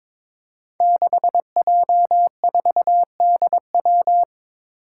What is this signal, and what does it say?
Morse code, heard as a keyed tone, about 22 wpm, 700 Hz. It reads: 6J4DW
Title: Five Challenging Letter-Number Combinations 22wpm